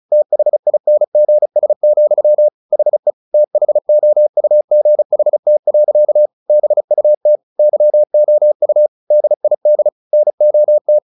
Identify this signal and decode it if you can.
Morse code, heard as a keyed tone, about 35 wpm, 600 Hz. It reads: THINGS, HE THOUGHT. BUT YOU DID NOT
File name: the old man and the sea 2 35wpm